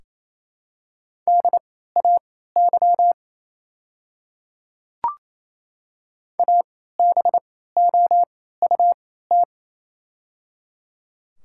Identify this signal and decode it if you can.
Morse code, heard as a keyed tone, about 28 wpm, spaced as for 16 wpm, 700 Hz. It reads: DAY ABOUT
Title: 3x Character Spacing - Top 100 Words - 28wpm